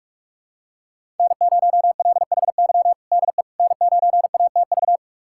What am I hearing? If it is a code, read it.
Morse code, heard as a keyed tone, about 42 wpm, 700 Hz. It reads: N0LHY DE N9ATV
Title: USA Callsign DE Callsign - 5 Characters - 45wpm